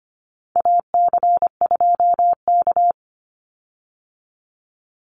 Morse code, heard as a keyed tone, about 25 wpm, 700 Hz. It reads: AC2K
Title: Call Signs Repeated 3x - 25wpm